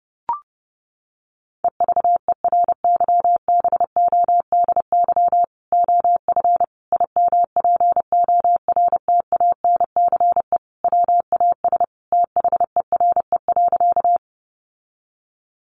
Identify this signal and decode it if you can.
Morse code, heard as a keyed tone, about 30 wpm, 700 Hz. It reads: EVERYBODY OF IMPORTANCE WAS THERE.
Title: Sentences from Top 1400 Words 30wpm